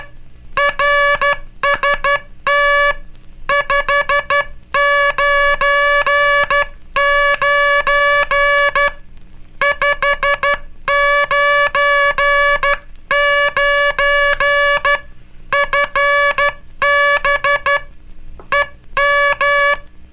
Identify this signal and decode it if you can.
Morse code, heard as a keyed tone, about 12 wpm, 1700 Hz. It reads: RST 599 599 FB EM
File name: cursos / Curso CW mp3 / CW Aula - 18